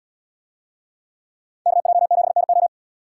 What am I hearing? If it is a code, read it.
Morse code, heard as a keyed tone, about 74 wpm, 700 Hz. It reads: KJ7NO